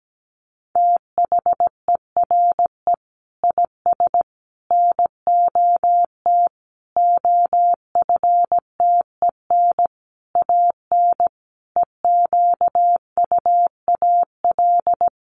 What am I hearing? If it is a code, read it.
Morse code, heard as a keyed tone, about 17 wpm, 700 Hz. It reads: THERE IS NOT OFTEN AN EQUAL
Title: Sentences from Top 1400 Words 17wpm